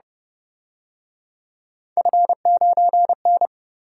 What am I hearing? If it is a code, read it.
Morse code, heard as a keyed tone, about 29 wpm, 700 Hz. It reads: F9N